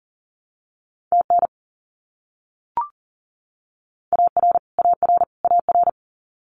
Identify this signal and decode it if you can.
Morse code, heard as a keyed tone, about 40 wpm, 700 Hz. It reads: TN AR AR AR
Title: US States Repeated 3x 40wpm